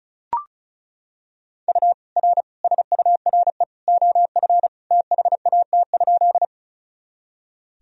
Morse code, heard as a keyed tone, about 36 wpm, 700 Hz. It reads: U R SURE OF THAT?
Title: Sentences from Top 300 Words and Abbreviations 35wpm